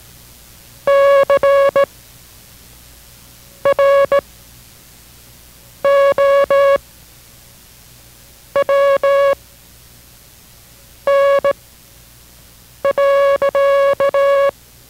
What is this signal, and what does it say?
Morse code, heard as a keyed tone, about 17 wpm, spaced as for 5 wpm, 560 Hz. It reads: CROWN.